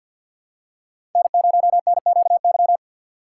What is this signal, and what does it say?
Morse code, heard as a keyed tone, about 52 wpm, 700 Hz. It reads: N0NQY
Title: USA Call Signs - 5 Characters - 6x Character Spacing - 50wpm